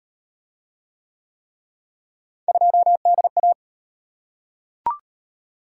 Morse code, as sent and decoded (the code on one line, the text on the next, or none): ..--- -.. .-
2DA